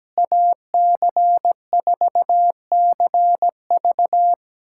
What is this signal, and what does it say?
Morse code, heard as a keyed tone, about 17 wpm, 700 Hz. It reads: AC4CV
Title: USA Callsign DE Callsign 17wpm